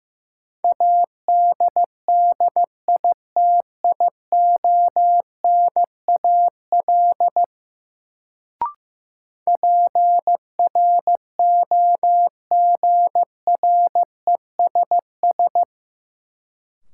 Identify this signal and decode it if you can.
Morse code, heard as a keyed tone, about 15 wpm, 700 Hz. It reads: ADDITIONAL PROGRESS
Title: Sentences from Top 1400 Words - New Word Drill Alt - 15wpm